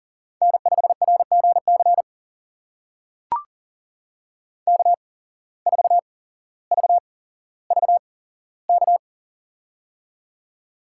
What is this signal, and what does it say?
Morse code, heard as a keyed tone, about 40 wpm, spaced as for 17 wpm, 700 Hz. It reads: N5RGC K4VVK
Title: USA Call Signs - 5 Characters - 8x Character Spacing - 40wpm